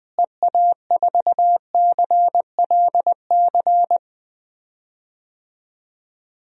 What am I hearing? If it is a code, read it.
Morse code, heard as a keyed tone, about 20 wpm, 700 Hz. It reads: EA4CLC
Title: DXCC-100 Call Signs - 6 Characters - Repeated 2x - 20wpm